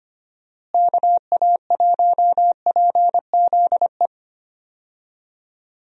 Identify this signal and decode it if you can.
Morse code, heard as a keyed tone, about 25 wpm, 700 Hz. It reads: KA1PZE
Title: USA Call Signs - 6 Characters - Repeated 3x - 25wpm